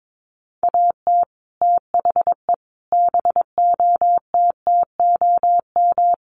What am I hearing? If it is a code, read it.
Morse code, heard as a keyed tone, about 22 wpm, 700 Hz. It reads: AT THE BOTTOM
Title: Sets of 5 Words Encore2 22wpm